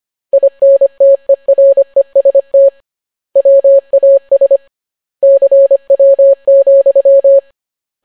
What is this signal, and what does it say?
Morse code, heard as a keyed tone, about 25 wpm, 550 Hz. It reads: INTEREST WAS CW,